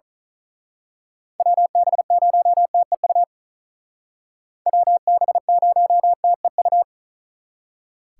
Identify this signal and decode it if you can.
Morse code, heard as a keyed tone, about 39 wpm, 700 Hz. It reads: WB0TEU WB0TEU
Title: Speed Racing - USA Call Signs 35wpm